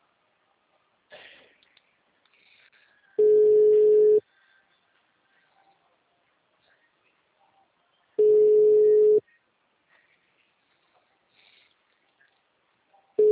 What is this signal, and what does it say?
Ringback tone (Europe, 425 Hz).